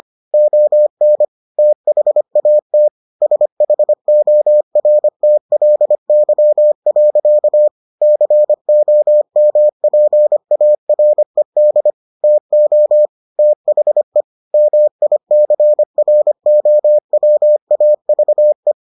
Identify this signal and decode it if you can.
Morse code, heard as a keyed tone, about 25 wpm, 600 Hz. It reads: ON THAT SHORTLY. COMPARED TO THE MICROWAVE